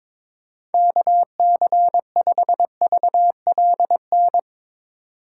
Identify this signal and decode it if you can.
Morse code, heard as a keyed tone, about 22 wpm, 700 Hz. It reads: KC5VLN
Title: USA Call Signs - 6 Characters - 2x Character Spacing - 22wpm